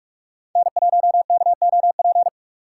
Morse code, heard as a keyed tone, about 44 wpm, 700 Hz. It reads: N1KOP
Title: USA Call Signs - 6x Character Spacing - 45wpm